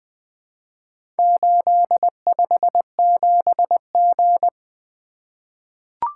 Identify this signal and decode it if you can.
Morse code, heard as a keyed tone, about 21 wpm, 700 Hz. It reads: 857G
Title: Four Letter-Number Combinations - 20wpm